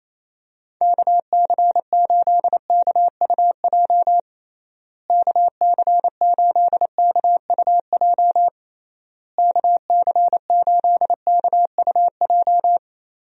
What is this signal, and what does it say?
Morse code, heard as a keyed tone, about 28 wpm, 700 Hz. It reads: KC8KUJ KC8KUJ KC8KUJ
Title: USA Call Signs - 6 Characters - Repeated 3x - 28wpm